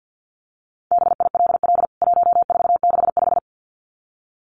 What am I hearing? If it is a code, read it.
Morse code, heard as a keyed tone, about 50 wpm, 700 Hz. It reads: BILL 1465
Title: CWT Exchange 50wpm